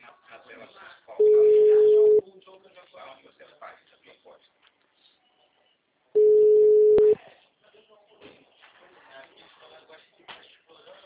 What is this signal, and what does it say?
Ringback tone (Europe, 425 Hz), with speech.
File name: out- -2016- - - .39725